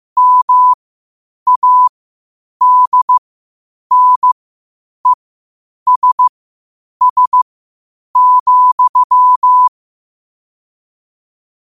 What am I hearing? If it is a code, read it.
Morse code, heard as a keyed tone, about 15 wpm, spaced as for 8 wpm, 1000 Hz. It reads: MADNESS,